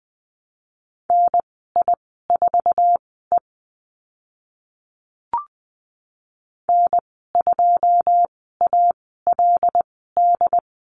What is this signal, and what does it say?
Morse code, heard as a keyed tone, about 20 wpm, spaced as for 14 wpm, 700 Hz. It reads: NI4E N2ALD
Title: 082 Lesson 14g - Number 2 - Call Signs 20wpm